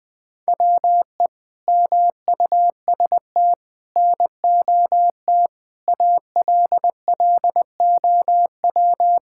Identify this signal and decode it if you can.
Morse code, heard as a keyed tone, about 21 wpm, 700 Hz. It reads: WE MUST NOT ALLOW
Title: Sentences from Top 1000 Words - Review Part 5 - 20wpm